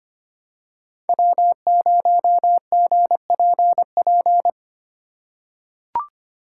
Morse code, heard as a keyed tone, about 25 wpm, 700 Hz. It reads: W0GPP
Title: USA Call Signs - 2x Character Spacing - 25wpm